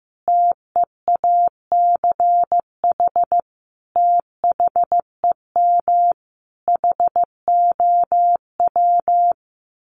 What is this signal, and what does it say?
Morse code, heard as a keyed tone, about 15 wpm, 700 Hz. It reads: TEACH THEM HOW
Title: Sentences from Top 1300 Words 15wpm